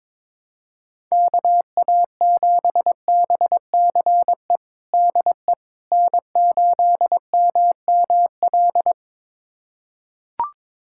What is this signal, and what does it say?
Morse code, heard as a keyed tone, about 22 wpm, 700 Hz. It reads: KA7BCE DE N8MML
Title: USA Callsign DE Callsign 22wpm